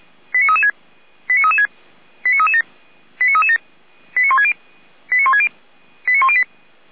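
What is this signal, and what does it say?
Five-tone selective call (ZVEI), heard as quick runs of five tones, tones of about 70 ms. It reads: 89336 89336 89337 89337 89160 89160 89108
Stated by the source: Five-tone ZVEI selective call received on an SDR.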